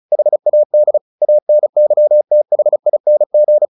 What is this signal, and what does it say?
Morse code, heard as a keyed tone, about 34 wpm, 600 Hz. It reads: HAD ANYTHING
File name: the old man and the sea 2 35wpm